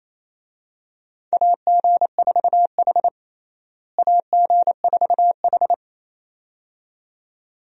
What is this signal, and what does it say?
Morse code, heard as a keyed tone, about 28 wpm, 700 Hz. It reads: AG4H AG4H
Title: USA Call Signs - 4 Characters - Repeated 2x - 28wpm